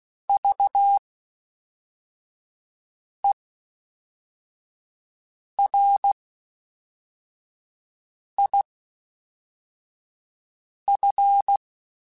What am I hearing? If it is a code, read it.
Morse code, heard as a keyed tone, about 16 wpm, spaced as for 4 wpm, 800 Hz. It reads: VERIF
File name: Special INGs / Special INGs-mp3s / Fast ing @10